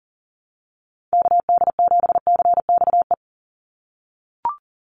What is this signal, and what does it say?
Morse code, heard as a keyed tone, about 40 wpm, 700 Hz. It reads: KD7CXE
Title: USA Call Signs 40wpm